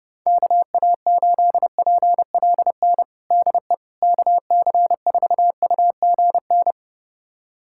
Morse code, heard as a keyed tone, about 32 wpm, 690 Hz. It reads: KA8PLN DE KC4UGN
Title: USA Callsign DE Callsign 30wpm